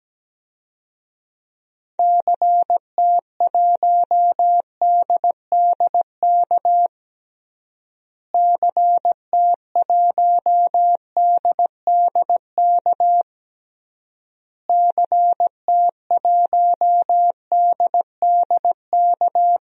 Morse code, heard as a keyed tone, about 17 wpm, 700 Hz. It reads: CT1DDK CT1DDK CT1DDK
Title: DXCC-100 Call Signs - 6 Characters - Repeated 3x - 17wpm